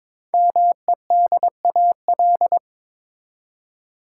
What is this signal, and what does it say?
Morse code, heard as a keyed tone, about 22 wpm, 700 Hz. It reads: MEDAL
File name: 5 Letter Words - First 3 Letters Given - 22wpm